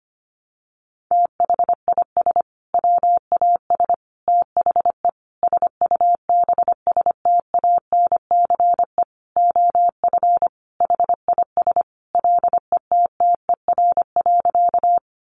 Morse code, heard as a keyed tone, about 25 wpm, 700 Hz. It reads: THIS WAS THE SUBSTANCE OF HIS LETTER.
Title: Sentences from Top 1500 Words 25wpm